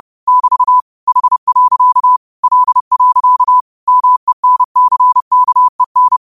Morse code, heard as a keyed tone, about 30 wpm, 1000 Hz. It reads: <BT> H. L. MENCKEN
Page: Quotes for Fri, 15 Aug 2025 in Morse Code at 30 words per minute.